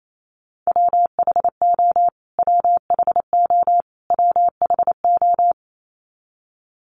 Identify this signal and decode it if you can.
Morse code, heard as a keyed tone, about 28 wpm, 700 Hz. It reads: WHO WHO WHO
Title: Top 500 Words Repeated 28wpm